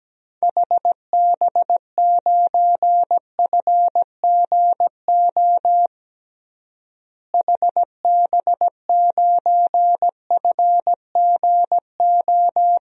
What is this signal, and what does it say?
Morse code, heard as a keyed tone, about 17 wpm, 700 Hz. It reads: HB9FGO HB9FGO
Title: DXCC-100 Call Signs - 6 Characters - Repeated 2x - 17wpm